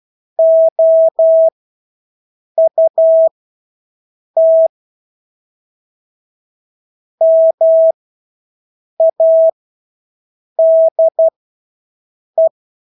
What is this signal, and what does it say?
Morse code, heard as a keyed tone, about 12 wpm, spaced as for 6 wpm, 650 Hz. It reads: OUT MADE
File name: Engelske ord 12-6 wpm